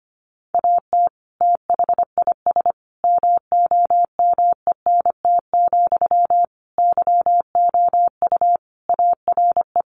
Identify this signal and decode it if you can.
Morse code, heard as a keyed tone, about 25 wpm, 700 Hz. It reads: AT THIS MOMENT, YOU ARE
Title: Sentences from Top 1000 Words 25wpm